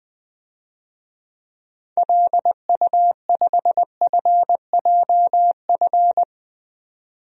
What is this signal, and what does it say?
Morse code, heard as a keyed tone, about 20 wpm, 700 Hz. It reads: LU5FJF
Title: DXCC-100 Call Signs - 6 Characters - 20wpm